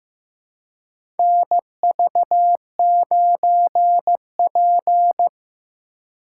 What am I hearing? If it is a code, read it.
Morse code, heard as a keyed tone, about 15 wpm, 700 Hz. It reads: NV9P
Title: USA Call Signs - 2x Character Spacing - 15wpm